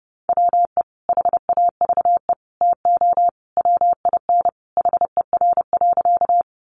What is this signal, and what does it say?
Morse code, heard as a keyed tone, about 30 wpm, 700 Hz. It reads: WE HAVE TO WIN HER.
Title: Sentences from Top 1500 Words 30wpm